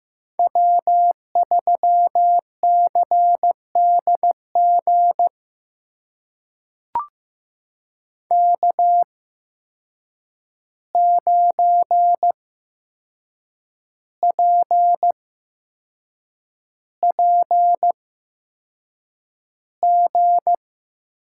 Morse code, heard as a keyed tone, about 15 wpm, spaced as for 6 wpm, 700 Hz. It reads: W3CDG K9PPG
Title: USA Call Signs - 5 Characters - 8x Character Spacing - 15wpm